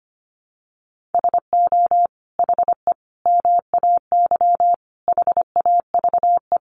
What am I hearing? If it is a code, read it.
Morse code, heard as a keyed tone, about 25 wpm, 700 Hz. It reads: SO HE MAY HAVE
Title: Sentences from Top 1500 Words 25wpm